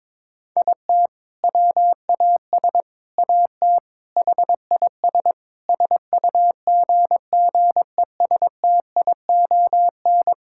Morse code, heard as a keyed tone, about 22 wpm, 700 Hz. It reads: IT WAS AT HIS SUGGESTION
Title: Sentences from Top 1600 Words 22wpm